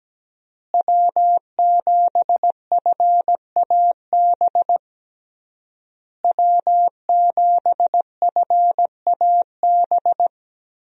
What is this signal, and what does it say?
Morse code, heard as a keyed tone, about 17 wpm, 700 Hz. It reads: W7FAB W7FAB
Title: USA Call Signs - 5 Characters - Repeated 3x - 17wpm